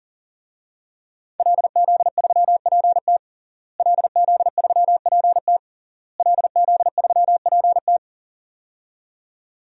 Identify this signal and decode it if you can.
Morse code, heard as a keyed tone, about 40 wpm, 700 Hz. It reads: LZ3PT LZ3PT LZ3PT
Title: DXCC-100 Call Signs - 5 Characters - Repeated 3x - 40wpm